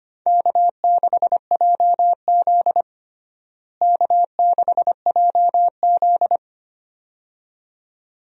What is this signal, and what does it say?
Morse code, heard as a keyed tone, about 25 wpm, 700 Hz. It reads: K6JZ K6JZ
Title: USA Call Signs - 4 Characters - Repeated 2x - 25wpm